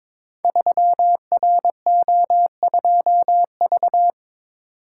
Morse code, heard as a keyed tone, about 23 wpm, 700 Hz. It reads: 3RO2V
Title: Code Groups - 5 Letter:Numbers - 22wpm